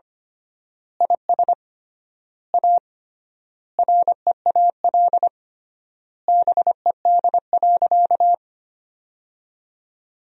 Morse code, heard as a keyed tone, about 25 wpm, 700 Hz. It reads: IS A REAL BED.
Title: 3x Word-Spacing - Sentences from Top 500 Words - 25wpm